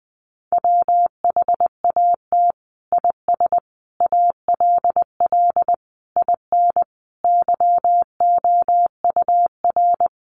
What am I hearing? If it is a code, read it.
Morse code, heard as a keyed tone, about 20 wpm, 700 Hz. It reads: WHAT IS ALL IN YOUR
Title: Sentences from Top 1200 Words 20wpm